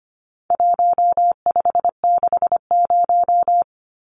Morse code, heard as a keyed tone, about 25 wpm, 700 Hz. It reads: 1560
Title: Numbers - Contest Style - 4-Digit 25wpm